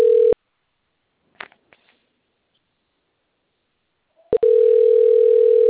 Ringback tone (North America), 440 + 480 Hz.